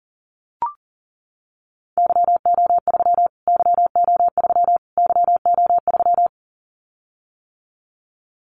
Morse code, heard as a keyed tone, about 37 wpm, 700 Hz. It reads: YO3 YO3 YO3
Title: Callsign Prefixes for Top 25 DXCC - Repeated 3x - 40wpm